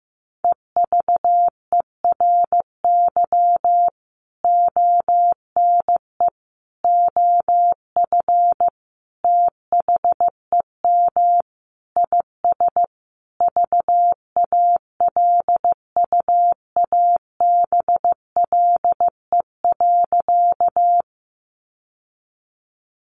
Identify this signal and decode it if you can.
Morse code, heard as a keyed tone, about 15 wpm, 700 Hz. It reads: EVERY ONE OF THEM IS VALUABLE.
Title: Sentences from Top 1100 Words 15wpm